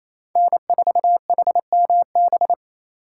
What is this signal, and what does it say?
Morse code, heard as a keyed tone, about 33 wpm, 700 Hz. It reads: N4HMB
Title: Speed Racing - USA 1x3 Call Signs 28wpm